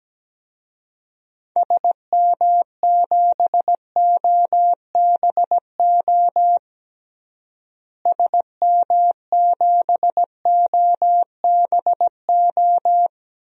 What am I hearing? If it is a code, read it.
Morse code, heard as a keyed tone, about 17 wpm, 700 Hz. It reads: SM7OBO SM7OBO
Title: DXCC-100 Call Signs - 6 Characters - Repeated 3x - 17wpm